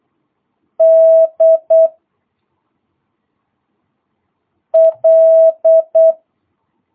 Morse code, heard as a keyed tone, about 8 wpm, 650 Hz. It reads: D L